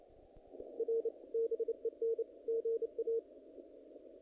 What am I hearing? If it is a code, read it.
Morse code, heard as a keyed tone, about 26 wpm, 460 Hz. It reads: R BEN GA
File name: contests / naqp-cw-k5tr-op-k5pi- -2000